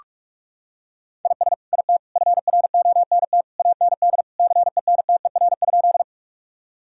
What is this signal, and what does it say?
Morse code, heard as a keyed tone, about 46 wpm, 700 Hz. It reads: IS IT FRONT AND CENTER?
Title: Sentences from Top 1000 Words - Review Part 2 - 45wpm